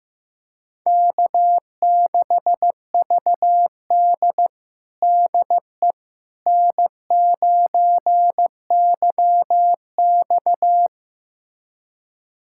Morse code, heard as a keyed tone, about 15 wpm, 700 Hz. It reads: K6VD DE N9YX
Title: USA Callsign DE Callsign - 4 Characters - 15wpm